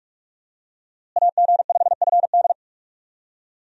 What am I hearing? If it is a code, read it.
Morse code, heard as a keyed tone, about 46 wpm, 700 Hz. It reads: AG5FD
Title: USA Call Signs - 5 Characters - Repeated 2x - 45wpm